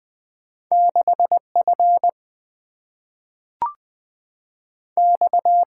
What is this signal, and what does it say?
Morse code, heard as a keyed tone, about 20 wpm, 700 Hz. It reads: 6F X
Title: Single-Letter Single-Number 20wpm